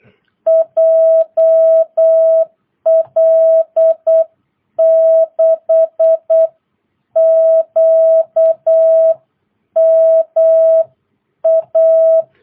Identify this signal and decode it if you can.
Morse code, heard as a keyed tone, about 8 wpm, 650 Hz. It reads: JL6QMA